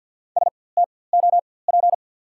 Morse code, heard as a keyed tone, about 48 wpm, spaced as for 23 wpm, 700 Hz. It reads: STOP
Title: 4x Character Spacing - Top 300 Words - 50wpm